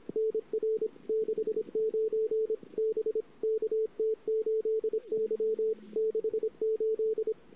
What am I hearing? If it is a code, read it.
Morse code, heard as a keyed tone, about 29 wpm, 440 Hz. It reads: NR 69 B KT8Y 68
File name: CW / 2008-ss-cw-k5tr-op-n5rz -22:00